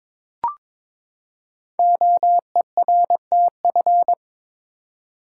Morse code, heard as a keyed tone, about 22 wpm, 700 Hz. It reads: OERTF
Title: Code Groups - 5 Letters - 22wpm